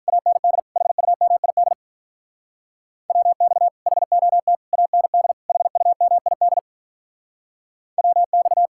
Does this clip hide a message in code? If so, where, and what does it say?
Yes: Morse code from 0.1 to 8.7 s, heard as a keyed tone, about 45 wpm, 700 Hz. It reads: AND HUMID WX HOT AND HUMID WX